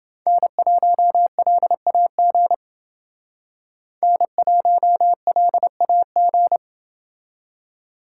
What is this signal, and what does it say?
Morse code, heard as a keyed tone, about 29 wpm, 700 Hz. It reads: N1LAG N1LAG